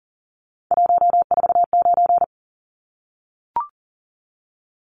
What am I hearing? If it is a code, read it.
Morse code, heard as a keyed tone, about 40 wpm, 700 Hz. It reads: 149
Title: Numbers - 3-Digit 40wpm